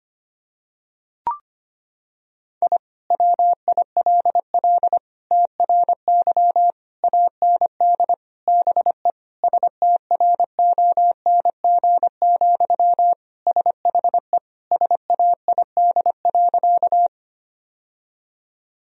Morse code, heard as a keyed tone, about 25 wpm, 700 Hz. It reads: I WILL TRY AND BE STRONG, SHE SAID.
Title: Sentences from Top 1000 Words - Review Part 3 - 25wpm